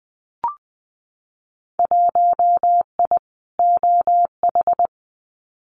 Morse code, heard as a keyed tone, about 20 wpm, 700 Hz. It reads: 1I OH
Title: Winter Field Day Exchanges 20wpm